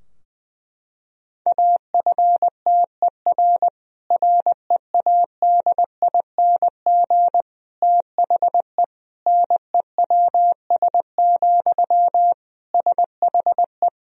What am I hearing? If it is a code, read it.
Morse code, heard as a keyed tone, about 21 wpm, 700 Hz. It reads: AFTER READING THE NEWS, SHE
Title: Sentences from Top 1000 Words - Review Part 3 - 20wpm